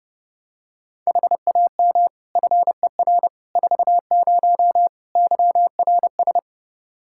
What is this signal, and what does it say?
Morse code, heard as a keyed tone, about 30 wpm, 700 Hz. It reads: HAM FER 40 YRS
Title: Practice QSO - Exchange 3 - More Info - 30wpm